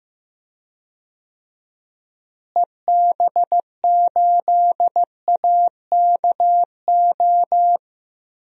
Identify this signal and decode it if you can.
Morse code, heard as a keyed tone, about 15 wpm, 700 Hz. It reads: EB8AKO
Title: DXCC-100 Call Signs - 6 Characters - Repeated 3x - 15wpm